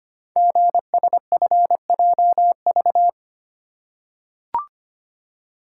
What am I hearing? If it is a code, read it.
Morse code, heard as a keyed tone, about 25 wpm, 700 Hz. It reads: GSFJV
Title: Code Groups - 5 Letters - 25wpm